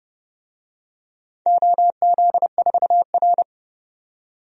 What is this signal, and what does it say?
Morse code, heard as a keyed tone, about 31 wpm, 700 Hz. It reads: OZ4R
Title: DXCC-100 Call Signs - 4 Characters - 30wpm